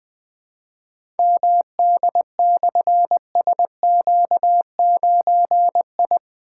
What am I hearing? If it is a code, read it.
Morse code, heard as a keyed tone, about 20 wpm, 700 Hz. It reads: MD/SQ9I
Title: Difficult Call Signs 20wpm